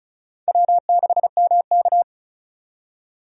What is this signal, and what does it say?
Morse code, heard as a keyed tone, about 36 wpm, 700 Hz. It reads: W6MK
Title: USA Call Signs - 4 Characters - 35wpm